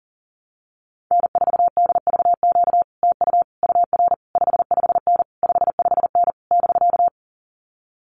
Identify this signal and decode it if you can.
Morse code, heard as a keyed tone, about 40 wpm, 700 Hz. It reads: N4DVQ TU UR 55N 55N <BK>
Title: Common QSO Phrases Encore 40wpm